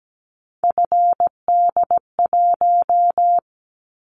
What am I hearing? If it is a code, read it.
Morse code, heard as a keyed tone, about 17 wpm, 700 Hz. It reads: FD1
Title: Callsign Prefixes for Top 50 DXCC - 17wpm